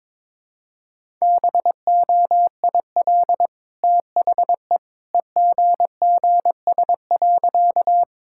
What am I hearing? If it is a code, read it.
Morse code, heard as a keyed tone, about 21 wpm, 690 Hz. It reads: BOIL THE EGGS.
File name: practice / 22wpm / Easy Sentences with Hints - 22wpm